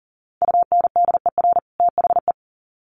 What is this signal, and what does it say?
Morse code, heard as a keyed tone, about 40 wpm, 700 Hz. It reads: UNDER THE
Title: Sets of 2 Words 40wpm